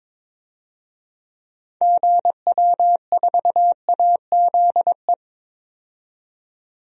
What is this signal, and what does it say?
Morse code, heard as a keyed tone, about 22 wpm, 700 Hz. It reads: GW4AZE
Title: DXCC-100 Call Signs - 6 Characters - 22wpm